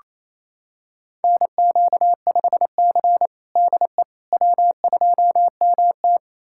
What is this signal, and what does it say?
Morse code, heard as a keyed tone, about 28 wpm, 700 Hz. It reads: NQ5C DE W2MT
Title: USA Callsign DE Callsign - 4 Characters - 28wpm